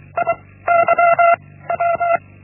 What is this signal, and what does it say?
Morse code, heard as a keyed tone, about 24 wpm, 680 Hz. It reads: I Y W
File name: Linksmakalnis-1